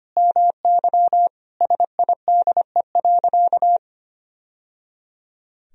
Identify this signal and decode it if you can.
Morse code, heard as a keyed tone, about 25 wpm, 700 Hz. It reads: MY SIDE.
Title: Sentences from Top 1000 Words - Review Part 1 - 25wpm